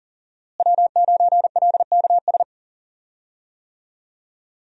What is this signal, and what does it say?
Morse code, heard as a keyed tone, about 37 wpm, 700 Hz. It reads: W9LKS
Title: USA Call Signs - 5 Characters - Repeated 2x - 40wpm